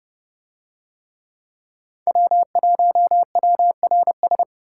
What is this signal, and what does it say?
Morse code, heard as a keyed tone, about 29 wpm, 700 Hz. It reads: W1WRS